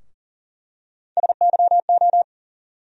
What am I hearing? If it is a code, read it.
Morse code, heard as a keyed tone, about 40 wpm, 700 Hz. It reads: SYO
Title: USA Call Sign Postfixes - 3 Letters 40wpm